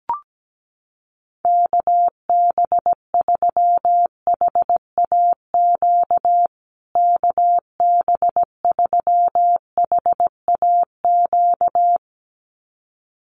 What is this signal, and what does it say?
Morse code, heard as a keyed tone, about 17 wpm, 700 Hz. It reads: KB3HAQ KB3HAQ
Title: USA Call Signs Repeated 2x 17wpm